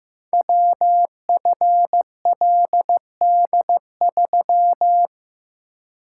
Morse code, heard as a keyed tone, about 15 wpm, 700 Hz. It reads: WFLD3
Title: Code Groups - 5 Letter:Numbers - 15wpm